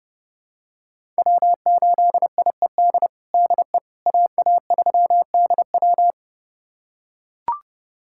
Morse code, heard as a keyed tone, about 30 wpm, 700 Hz. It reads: W8IED DE AA3DW
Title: USA Callsign DE Callsign - 5 Characters - 30wpm